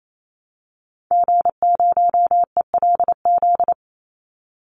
Morse code, heard as a keyed tone, about 28 wpm, 700 Hz. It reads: G0ELZ
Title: Call Signs - 28wpm